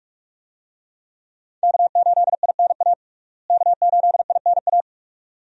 Morse code, heard as a keyed tone, about 42 wpm, 700 Hz. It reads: K8INA K8INA
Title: USA Call Signs - 5 Characters - Repeated 2x - 45wpm